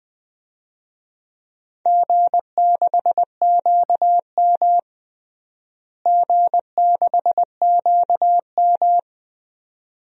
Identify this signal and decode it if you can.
Morse code, heard as a keyed tone, about 20 wpm, 700 Hz. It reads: G6QM G6QM
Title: DXCC-100 Call Signs - 4 Characters - Repeated 2x - 20wpm